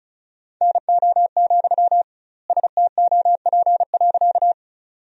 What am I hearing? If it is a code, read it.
Morse code, heard as a keyed tone, about 36 wpm, 700 Hz. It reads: NO, STOP.
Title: 2x Word-Spacing - Sentences from Top 300 Words - 35wpm